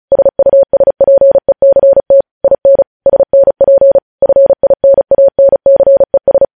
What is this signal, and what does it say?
Morse code, heard as a keyed tone, about 36 wpm, 550 Hz. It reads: SUSPECT IN SNP FINANCES
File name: podcast / News headlines-1x-35x35